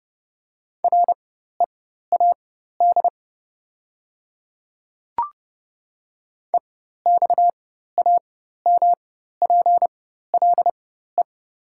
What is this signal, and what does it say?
Morse code, heard as a keyed tone, about 29 wpm, spaced as for 14 wpm, 690 Hz. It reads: READ EXAMPLE
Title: 4x Character Spacing - Top 300 Words - 30wpm